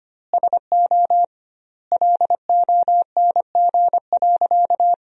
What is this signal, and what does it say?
Morse code, heard as a keyed tone, about 25 wpm, 700 Hz. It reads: SO LONG.
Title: 2x Word-Spacing - Sentences from Top 100 Words - 25wpm